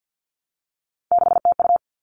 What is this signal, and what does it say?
Morse code, heard as a keyed tone, about 50 wpm, 700 Hz. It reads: 6TU
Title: All Cut Numbers - 3-Digit 50wpm